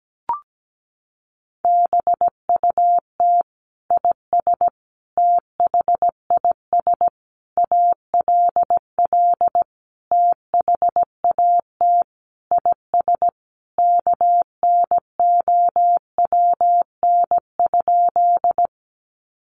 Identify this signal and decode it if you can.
Morse code, heard as a keyed tone, about 17 wpm, 700 Hz. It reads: BUT IS THIS ALL THAT IS KNOWN?
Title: Sentences from Top 600 Words 17wpm